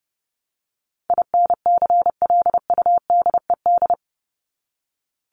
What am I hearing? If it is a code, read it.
Morse code, heard as a keyed tone, about 30 wpm, 700 Hz. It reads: INCLUDED
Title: ED Suffix - Top 100 Words with Base Words 30wpm